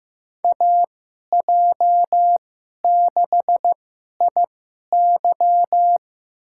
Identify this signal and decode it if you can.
Morse code, heard as a keyed tone, about 15 wpm, spaced as for 11 wpm, 700 Hz. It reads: AJ6IY
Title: USA Call Signs - 5 Characters - 2x Character Spacing - 15wpm